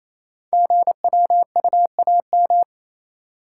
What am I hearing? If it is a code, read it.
Morse code, heard as a keyed tone, about 28 wpm, 700 Hz. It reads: GWUAM
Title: Code Groups - 5 Letters - 28wpm